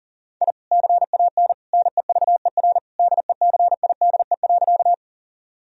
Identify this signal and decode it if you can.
Morse code, heard as a keyed tone, about 40 wpm, 700 Hz. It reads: I CAN NEVER DECIDE.
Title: Sentences from Top 1000 Words - Review Part 3 - 40wpm